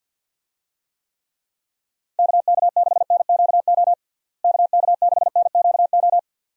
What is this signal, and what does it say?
Morse code, heard as a keyed tone, about 52 wpm, 700 Hz. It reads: KK6NQO KK6NQO
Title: USA Call Signs - 6 Characters - Repeated 2x - 50wpm